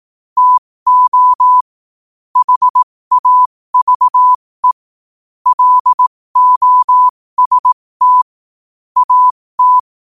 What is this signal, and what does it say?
Morse code, heard as a keyed tone, about 18 wpm, spaced as for 15 wpm, 1000 Hz. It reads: TO HAVE LOST AT
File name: qotdfiles / 15wpm